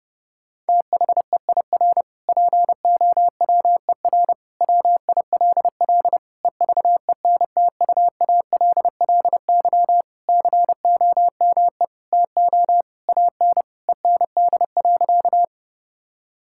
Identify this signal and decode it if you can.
Morse code, heard as a keyed tone, about 30 wpm, 700 Hz. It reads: THEIR POWER WILL EVENTUALLY COME TO AN END.
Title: Sentences from Top 1000 Words - Review Part 6 - 30wpm